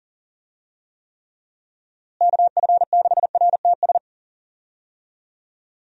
Morse code, heard as a keyed tone, about 40 wpm, 700 Hz. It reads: KF6RTS
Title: USA Call Signs - 6 Characters - Repeated 2x - 40wpm